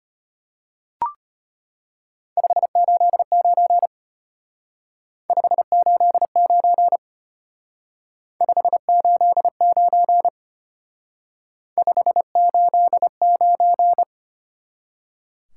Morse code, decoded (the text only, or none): 589 589 589 589